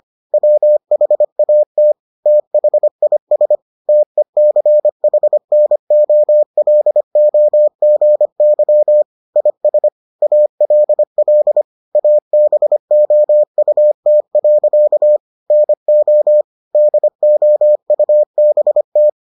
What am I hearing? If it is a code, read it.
Morse code, heard as a keyed tone, about 25 wpm, 600 Hz. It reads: WHAT THIS TECHNOLOGY IS ALL ABOUT. NO DOUBT